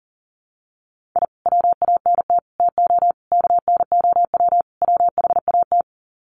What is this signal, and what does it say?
Morse code, heard as a keyed tone, about 40 wpm, 700 Hz. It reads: I WANT TO KNOW WHAT
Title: Sets of 5 Words Encore2 40wpm